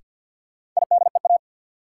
Morse code, heard as a keyed tone, about 52 wpm, 700 Hz. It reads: IDEA
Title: Sets of 3 Words - New Word Drill - 50wpm